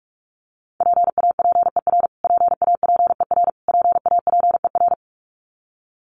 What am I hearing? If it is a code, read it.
Morse code, heard as a keyed tone, about 44 wpm, 700 Hz. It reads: PAPER PAPER PAPER
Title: Top 300 Words Repeated 45wpm